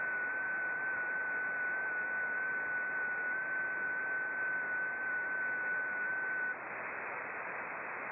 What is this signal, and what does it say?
FT8.
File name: contests / sept-vhf-k5tr- -0300